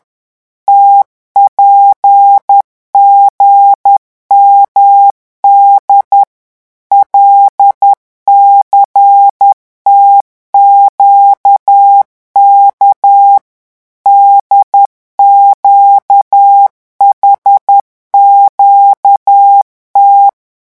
Morse code, heard as a keyed tone, about 10 wpm, 800 Hz. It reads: TPGMD LCTQK DQHQT
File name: morse mp3 / revision11 lettres vitesse 10